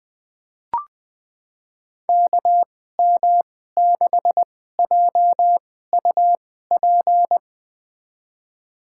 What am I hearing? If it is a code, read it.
Morse code, heard as a keyed tone, about 20 wpm, spaced as for 14 wpm, 700 Hz. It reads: KM6JUP
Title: USA Call Signs - 6 Characters - 2x Character Spacing - 20wpm